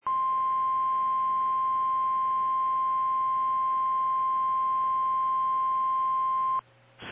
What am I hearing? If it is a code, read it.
Emergency Alert System (SAME) tones (the 1050 Hz attention tone). It is heard from 0.0 to 6.6 s.
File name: eas / KKPRKXPN / KKPRKXPN